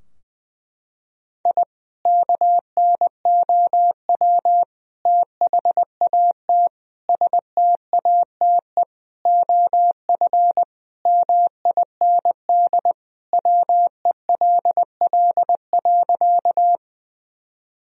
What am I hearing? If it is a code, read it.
Morse code, heard as a keyed tone, about 20 wpm, 700 Hz. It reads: I KNOW THAT STATE OF MIND WELL.
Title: Sentences from Top 1000 Words - Review Part 2 - 20wpm